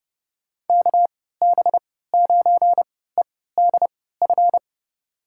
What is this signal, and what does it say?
Morse code, heard as a keyed tone, about 31 wpm, spaced as for 17 wpm, 700 Hz. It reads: KB9EDF
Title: USA Call Signs - 6 Characters - 3x Character Spacing - 30wpm